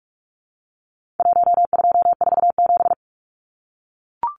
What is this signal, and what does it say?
Morse code, heard as a keyed tone, about 44 wpm, 700 Hz. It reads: 1247
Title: Numbers - 4-Digit 45wpm